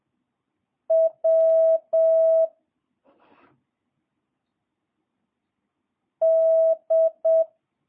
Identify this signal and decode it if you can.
Morse code, heard as a keyed tone, about 7 wpm, 650 Hz. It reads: W D
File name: jh8cbh / 2022cw